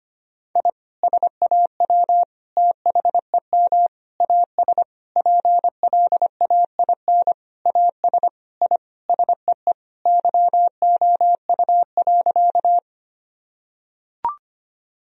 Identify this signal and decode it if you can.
Morse code, heard as a keyed tone, about 25 wpm, 700 Hz. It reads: I SAW THEM AS PLAIN AS I SEE YOU.
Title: Sentences from Top 1000 Words - Review Part 2 - 25wpm